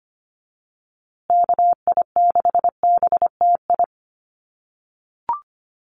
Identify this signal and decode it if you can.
Morse code, heard as a keyed tone, about 25 wpm, 700 Hz. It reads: KI6BTI
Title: USA Call Signs Repeated 2x 25wpm